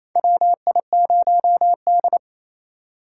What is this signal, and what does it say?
Morse code, heard as a keyed tone, about 28 wpm, 700 Hz. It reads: WI0D
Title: USA Call Signs - 8x Character Spacing - 28wpm